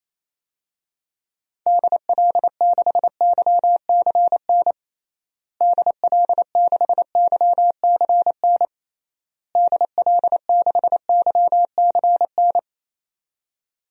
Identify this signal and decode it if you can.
Morse code, heard as a keyed tone, about 27 wpm, 700 Hz. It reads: DL6YCN DL6YCN DL6YCN
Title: DXCC-100 Call Signs - 6 Characters - Repeated 3x - 28wpm